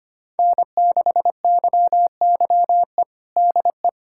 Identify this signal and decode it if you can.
Morse code, heard as a keyed tone, about 25 wpm, 700 Hz. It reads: N6YYE DE
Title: USA Callsign DE Callsign - 5 Characters - 25wpm